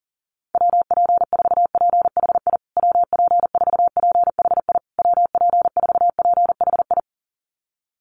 Morse code, heard as a keyed tone, about 40 wpm, 700 Hz. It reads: WP4PHI WP4PHI WP4PHI
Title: USA Call Signs Repeated 3x 40wpm